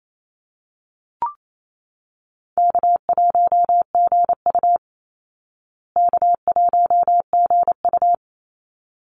Morse code, heard as a keyed tone, about 28 wpm, 700 Hz. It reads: K1GU K1GU
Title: Easy Call Signs Repeated 2x - 28wpm